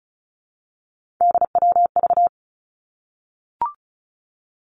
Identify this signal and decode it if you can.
Morse code, heard as a keyed tone, about 34 wpm, 700 Hz. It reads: DWV
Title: Three Challenging Letter-Number Combinations 35wpm